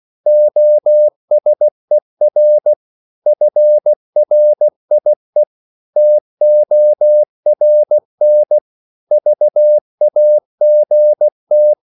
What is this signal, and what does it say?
Morse code, heard as a keyed tone, about 16 wpm, 600 Hz. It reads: OSER FRIE TORN VAGT
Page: Korte danske ord 16wpm